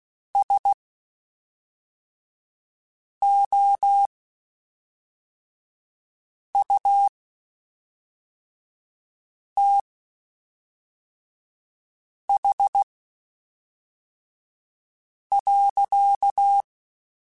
Morse code, heard as a keyed tone, about 16 wpm, spaced as for 3 wpm, 800 Hz. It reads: SOUTH.
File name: Spaced Characters / Spaced Characters-mp3s / Weather 4 21 sp @5